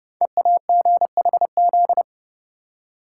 Morse code, heard as a keyed tone, about 30 wpm, 700 Hz. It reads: EAGHZ
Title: Code Groups - 5 Letters - 30wpm